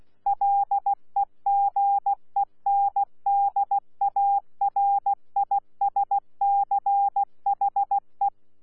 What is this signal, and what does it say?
Morse code, heard as a keyed tone, about 16 wpm, 800 Hz. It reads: LEGENDARISCHE
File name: Z PI4VRZ / PI4VRZ 14-jun-2025